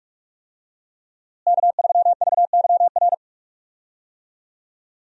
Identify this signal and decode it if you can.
Morse code, heard as a keyed tone, about 46 wpm, 700 Hz. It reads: K3VYR
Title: USA Call Signs - 5 Characters - Repeated 3x - 45wpm